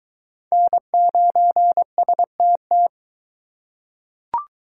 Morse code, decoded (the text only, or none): N9STT